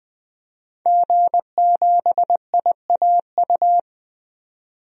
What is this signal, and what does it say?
Morse code, heard as a keyed tone, about 20 wpm, 700 Hz. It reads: G7IAU
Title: DXCC-100 Call Signs - 5 Characters - 20wpm